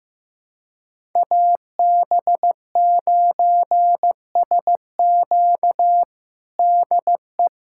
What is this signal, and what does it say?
Morse code, heard as a keyed tone, about 15 wpm, 700 Hz. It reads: AB9SQ DE
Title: USA Callsign DE Callsign 15wpm